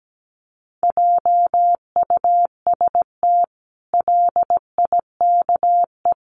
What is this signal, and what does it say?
Morse code, heard as a keyed tone, about 17 wpm, 700 Hz. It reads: JUST LIKE
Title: Sentences from Top 1200 Words 17wpm